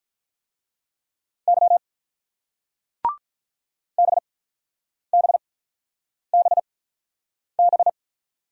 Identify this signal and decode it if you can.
Morse code, heard as a keyed tone, about 49 wpm, spaced as for 9 wpm, 700 Hz. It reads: Y BBBB